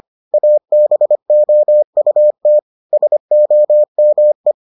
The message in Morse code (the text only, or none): ABOUT SOME